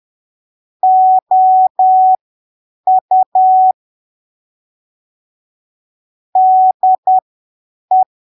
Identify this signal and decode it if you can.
Morse code, heard as a keyed tone, about 10 wpm, spaced as for 7 wpm, 750 Hz. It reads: OU DE
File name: arquivos / texto / texto 09 10ppm